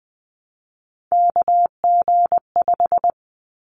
Morse code, heard as a keyed tone, about 20 wpm, 700 Hz. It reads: KG5
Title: Callsign Prefixes for Top 25 DXCC - 20wpm